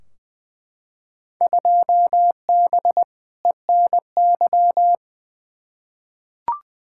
Morse code, heard as a keyed tone, about 21 wpm, 700 Hz. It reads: 2B ENY
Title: Field Day Exchanges Encore - 20wpm